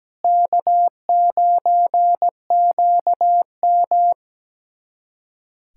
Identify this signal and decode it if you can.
Morse code, heard as a keyed tone, about 17 wpm, 700 Hz. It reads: K9QM
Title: USA Callsign DE Callsign - 4 Characters - 17wpm